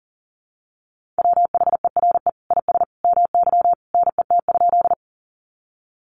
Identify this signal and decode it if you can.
Morse code, heard as a keyed tone, about 37 wpm, 700 Hz. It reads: WHERE IS MY NET?
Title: Sentences from Top 1400 Words 40wpm